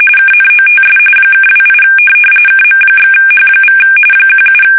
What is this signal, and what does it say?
RTTY, about 45.45 baud. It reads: KNOTS (28 KM/H= 17
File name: fsk850 / USS-Massachusetts